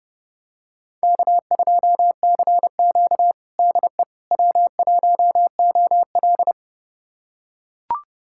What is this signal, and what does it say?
Morse code, heard as a keyed tone, about 32 wpm, 700 Hz. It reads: K2CQ DE W1OL
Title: USA Callsign DE Callsign - 4 Characters - 30wpm